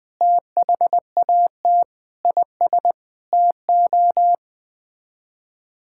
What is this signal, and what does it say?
Morse code, heard as a keyed tone, about 20 wpm, 700 Hz. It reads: THAT IS TO
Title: Sets of 8 Words - 20wpm